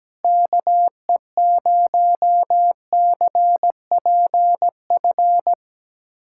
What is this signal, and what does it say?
Morse code, heard as a keyed tone, about 17 wpm, 700 Hz. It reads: KE0CPF
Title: USA Callsign DE Callsign - 6 Characters - 17wpm